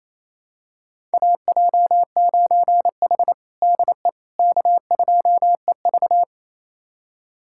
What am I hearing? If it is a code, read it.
Morse code, heard as a keyed tone, about 28 wpm, 700 Hz. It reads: AJ9H DE K2EV
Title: USA Callsign DE Callsign - 4 Characters - 28wpm